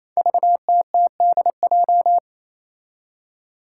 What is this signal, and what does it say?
Morse code, heard as a keyed tone, about 28 wpm, 700 Hz. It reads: VTTDJ
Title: Five Challenging Letter-Number Combinations 28wpm